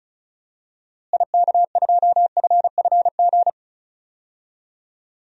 Morse code, heard as a keyed tone, about 34 wpm, 700 Hz. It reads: IK2FFG
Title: DXCC-100 Call Signs - 6 Characters - Repeated 2x - 35wpm